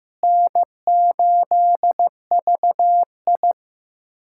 Morse code, heard as a keyed tone, about 15 wpm, 700 Hz. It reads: N8VI
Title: USA Callsign DE Callsign 15wpm